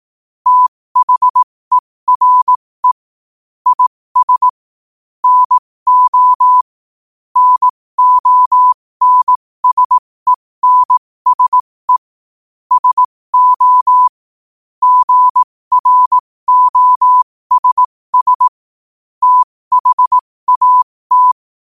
Morse code, heard as a keyed tone, about 18 wpm, spaced as for 15 wpm, 1000 Hz. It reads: THERE IS NO NONSENSE SO GROSS THAT
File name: qotdfiles / 15wpm